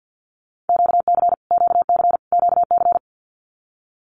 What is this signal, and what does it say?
Morse code, heard as a keyed tone, about 50 wpm, 700 Hz. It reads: QC QC QC
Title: Canadian Provinces and Territories - Repeated 3x - 50wpm